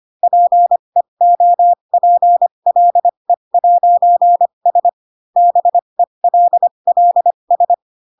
Morse code, heard as a keyed tone, about 25 wpm, spaced as for 22 wpm, 700 Hz. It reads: PEOPLE'S BELLS
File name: Pinocchio / Pinocchio ch6 18 25 ik1hge